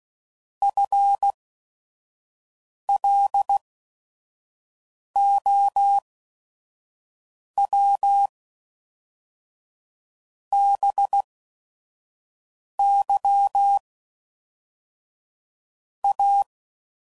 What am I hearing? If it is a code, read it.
Morse code, heard as a keyed tone, about 16 wpm, spaced as for 5 wpm, 800 Hz. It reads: FLOW BY A
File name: CW Code Groups @5WPM / Technical Subjects / XL